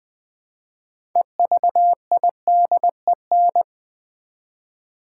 Morse code, heard as a keyed tone, about 20 wpm, 700 Hz. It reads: EVIDEN
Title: Build A Word - 3 Letters At A Time - 8-Letter Words - 20wpm